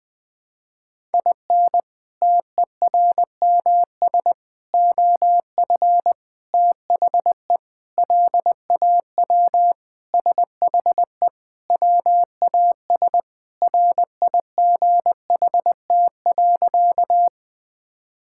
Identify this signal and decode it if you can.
Morse code, heard as a keyed tone, about 20 wpm, 700 Hz. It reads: IN TERMS OF THE LAW SHE WAS RIGHT.
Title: Sentences with 4-Word Phrases - 20wpm